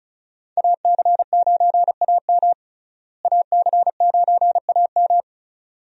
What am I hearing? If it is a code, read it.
Morse code, heard as a keyed tone, about 36 wpm, 700 Hz. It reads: AC9AM AC9AM
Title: USA Call Signs - 5 Characters - Repeated 2x - 35wpm